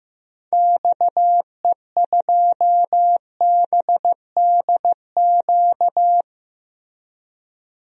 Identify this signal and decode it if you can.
Morse code, heard as a keyed tone, about 15 wpm, 700 Hz. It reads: XE2BDQ
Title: DXCC-100 Call Signs - 6 Characters - Repeated 2x - 15wpm